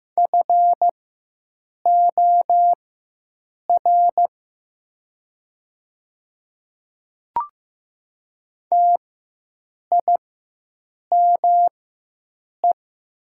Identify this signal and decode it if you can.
Morse code, heard as a keyed tone, about 15 wpm, spaced as for 7 wpm, 700 Hz. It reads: FOR TIME
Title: 4x Character Spacing - Top 200 Words - 15wpm